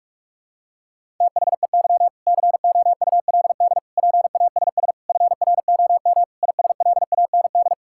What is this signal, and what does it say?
Morse code, heard as a keyed tone, about 41 wpm, 700 Hz. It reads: THEY COULD PASS FROM ISLAND
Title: Sentences from Top 1000 Words - Review Part 6 - 45wpm